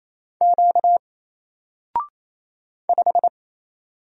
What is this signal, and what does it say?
Morse code, heard as a keyed tone, about 27 wpm, 700 Hz. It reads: Q 5
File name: Single-Letter-Number-28wpm